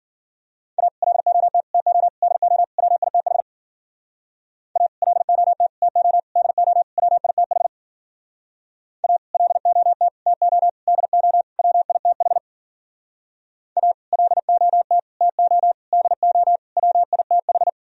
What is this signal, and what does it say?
Morse code, heard as a keyed tone, about 50 wpm, 700 Hz. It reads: A LOT TO DO WITH A LOT TO DO WITH A LOT TO DO WITH A LOT TO DO WITH
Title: Speed-Racing - Sets of 5 Words 40wpm